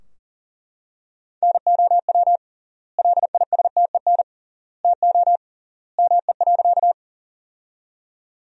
Morse code, heard as a keyed tone, about 40 wpm, 700 Hz. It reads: NOW LISTEN TO ME.
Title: 3x Word-Spacing - Sentences from Top 400 Words - 40wpm